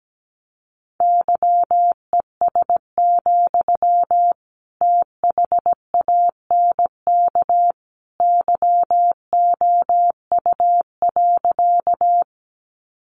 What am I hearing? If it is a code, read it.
Morse code, heard as a keyed tone, about 17 wpm, 700 Hz. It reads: YES, THANK YOU.
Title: Sentences from Top 600 Words 17wpm